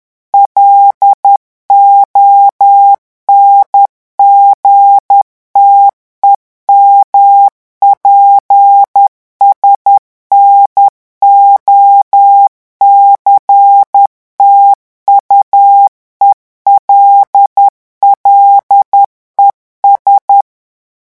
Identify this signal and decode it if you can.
Morse code, heard as a keyed tone, about 10 wpm, 800 Hz. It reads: LONGTEMPS NOCTUELLES
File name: morse mp3 / revision9 citations humoristiques vitesse 10 mots minute